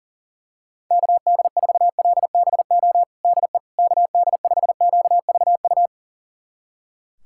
Morse code, heard as a keyed tone, about 40 wpm, 700 Hz. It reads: KD4LBO DE KD5QVU
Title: USA Callsign DE Callsign - 6 Characters - 40wpm